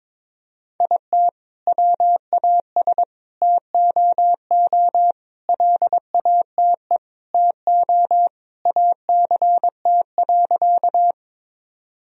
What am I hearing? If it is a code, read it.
Morse code, heard as a keyed tone, about 21 wpm, 700 Hz. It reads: IT WAS TOO LATE TO ACT.
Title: Sentences from Top 1000 Words - Review Part 1 - 22wpm